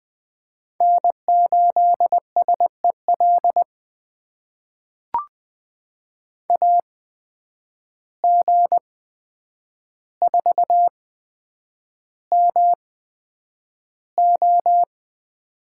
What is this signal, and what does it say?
Morse code, heard as a keyed tone, about 20 wpm, spaced as for 9 wpm, 700 Hz. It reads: N8SEL AG4MO
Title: USA Call Signs - 8x Character Spacing - 20wpm